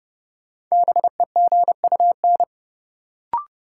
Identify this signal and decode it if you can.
Morse code, heard as a keyed tone, about 30 wpm, 700 Hz. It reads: BEGUN
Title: Sentences from Top 1000 Words - Review Part 6 - New Word Drill - 30wpm